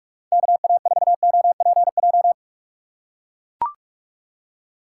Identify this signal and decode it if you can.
Morse code, heard as a keyed tone, about 43 wpm, 700 Hz. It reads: KA4OPJ
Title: USA Call Signs - 6 Characters - 6x Character Spacing - 45wpm